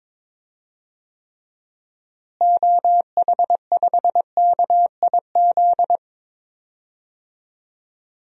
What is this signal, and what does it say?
Morse code, heard as a keyed tone, about 22 wpm, 700 Hz. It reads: OH5KIZ
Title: DXCC-100 Call Signs - 6 Characters - Repeated 3x - 22wpm